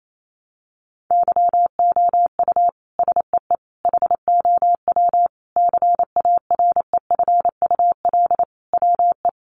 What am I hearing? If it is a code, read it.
Morse code, heard as a keyed tone, about 28 wpm, 700 Hz. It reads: YOU SEE HOW CAREFUL WE
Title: A Sound Of Thunder 28wpm